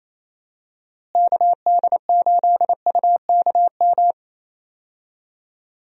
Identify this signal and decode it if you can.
Morse code, heard as a keyed tone, about 28 wpm, 700 Hz. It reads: KD8UKM
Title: USA Call Signs - 6 Characters - Repeated 2x - 28wpm